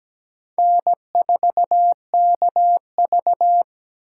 Morse code, heard as a keyed tone, about 17 wpm, 700 Hz. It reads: N4KV
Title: USA Call Signs - 4 Characters - 3x Character Spacing - 17wpm